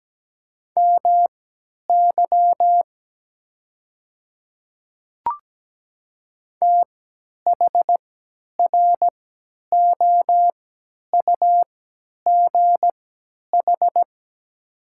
Morse code, heard as a keyed tone, about 17 wpm, spaced as for 10 wpm, 700 Hz. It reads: MY THROUGH
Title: 3x Character Spacing - Top 200 Words - 17wpm